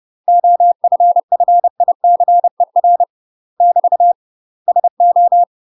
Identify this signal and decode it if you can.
Morse code, heard as a keyed tone, about 30 wpm, 700 Hz. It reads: OFFICER <BT> SO
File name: UK MASTER QSOs TRAINING/30WPM